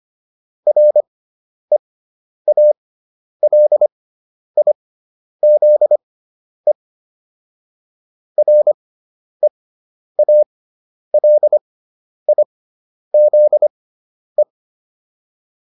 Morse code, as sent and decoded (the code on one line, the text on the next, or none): .-. .- .-.. .. --.. / .-. .- .-.. .. --.. .
REALIZE REALIZE